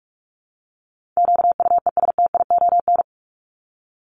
Morse code, heard as a keyed tone, about 44 wpm, 700 Hz. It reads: QUESTION
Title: ES:S Suffix - Top 100 Words with Base Words 45wpm